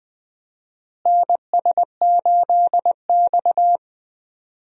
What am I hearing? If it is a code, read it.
Morse code, heard as a keyed tone, about 21 wpm, 700 Hz. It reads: NS8X
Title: USA Call Signs - 4 Characters - Repeated 2x - 20wpm